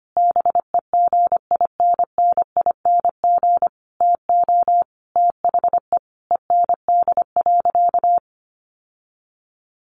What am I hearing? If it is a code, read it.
Morse code, heard as a keyed tone, about 25 wpm, 700 Hz. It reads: BEGINNING TO THE END.
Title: Sentences from Top 1300 Words 25wpm